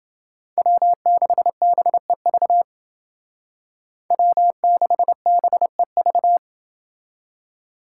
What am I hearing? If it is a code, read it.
Morse code, heard as a keyed tone, about 28 wpm, 700 Hz. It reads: W6BEV W6BEV